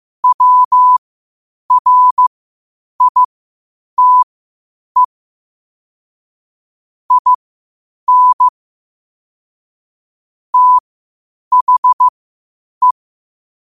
Morse code, heard as a keyed tone, about 15 wpm, spaced as for 8 wpm, 1000 Hz. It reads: WRITE IN THE
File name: qotdfiles / 8wpm